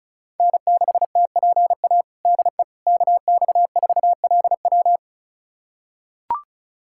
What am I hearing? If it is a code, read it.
Morse code, heard as a keyed tone, about 36 wpm, 700 Hz. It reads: N6TPA DE KX4LW
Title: USA Callsign DE Callsign - 5 Characters - 35wpm